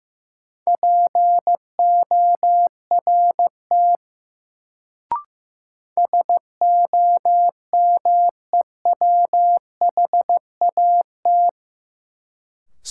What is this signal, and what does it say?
Morse code, heard as a keyed tone, about 15 wpm, 700 Hz. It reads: PORT SOMEWHAT
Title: Sentences from Top 1000 Words - Review Part 1 - New Word Drill - 15wpm